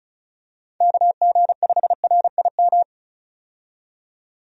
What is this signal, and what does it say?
Morse code, heard as a keyed tone, about 36 wpm, 700 Hz. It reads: KG5RIM
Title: USA Call Signs - 6 Characters - 35wpm